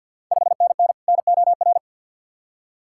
Morse code, heard as a keyed tone, about 52 wpm, 690 Hz. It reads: 5NN NOR
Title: Ontario QSO Party Exchange 50wpm